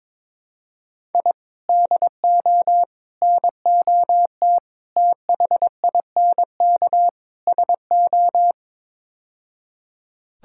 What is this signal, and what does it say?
Morse code, heard as a keyed tone, about 22 wpm, 700 Hz. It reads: I DO NOT THINK SO
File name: practice / 22wpm / Build A Phrase - 5-Word Phrases - 22wpm